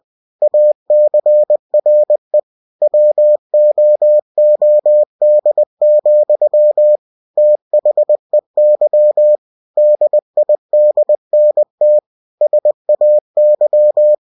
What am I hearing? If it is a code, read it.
Morse code, heard as a keyed tone, about 20 wpm, 600 Hz. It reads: ACRE WOOD, THEY DIDNT SAY
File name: cw-books / winnie-the-pooh / winnie-20wpm-page-0034